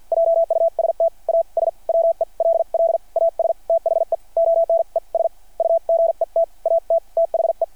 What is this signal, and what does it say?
Morse code, heard as a keyed tone, about 53 wpm, spaced as for 43 wpm, 660 Hz. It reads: JUST AS WELL AS THE ONES U GET AT THE
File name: archived / CRSnet